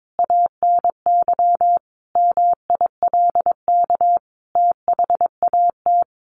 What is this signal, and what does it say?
Morse code, heard as a keyed tone, about 22 wpm, 700 Hz. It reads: ANY MILK THAT
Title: Sentences from Top 1500 Words 22wpm